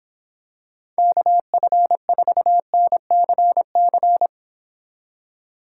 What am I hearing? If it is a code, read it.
Morse code, heard as a keyed tone, about 26 wpm, 700 Hz. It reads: KF4NCC